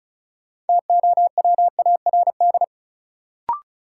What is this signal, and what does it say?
Morse code, heard as a keyed tone, about 36 wpm, 700 Hz. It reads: TOWARD
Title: Sentences from Top 1000 Words - Review Part 6 - New Word Drill - 35wpm